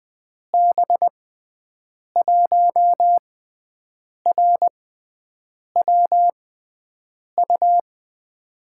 Morse code, heard as a keyed tone, about 20 wpm, spaced as for 7 wpm, 700 Hz. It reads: B1RWU
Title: USA Call Signs - 6 Characters - 6x Character Spacing - 20wpm